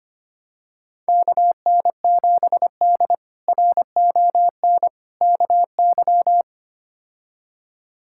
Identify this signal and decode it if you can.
Morse code, heard as a keyed tone, about 25 wpm, 700 Hz. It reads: KN7D RON KY
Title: K1USN SST QSO Practice - 25wpm